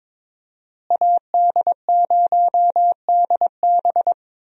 Morse code, heard as a keyed tone, about 22 wpm, 700 Hz. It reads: AD0DB
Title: USA Call Signs - 5 Characters - Repeated 2x - 22wpm